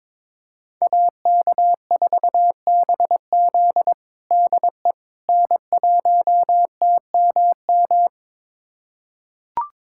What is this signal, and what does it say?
Morse code, heard as a keyed tone, about 22 wpm, 700 Hz. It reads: AK4BZ DE N1TMM
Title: USA Callsign DE Callsign - 5 Characters - 22wpm